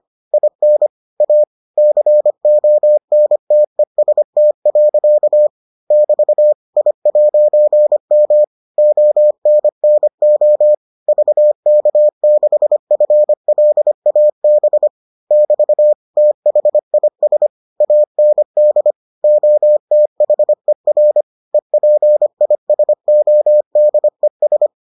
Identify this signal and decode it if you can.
Morse code, heard as a keyed tone, about 25 wpm, 600 Hz. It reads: IN A CONTEST. <BT> I'M ONNO VK6FLAB <BT> THIS AND OTHER EPISODES